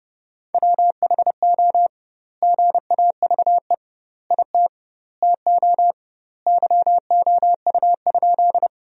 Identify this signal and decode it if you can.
Morse code, heard as a keyed tone, about 30 wpm, 700 Hz. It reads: WHO GAVE IT TO YOU?
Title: 2x Word-Spacing - Sentences from Top 500 Words - 30wpm